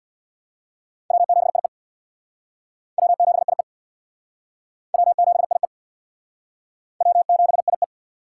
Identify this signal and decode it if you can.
Morse code, heard as a keyed tone, about 62 wpm, 700 Hz. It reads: W7IE W7IE W7IE W7IE
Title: Speed Racing - USA Call Signs 50wpm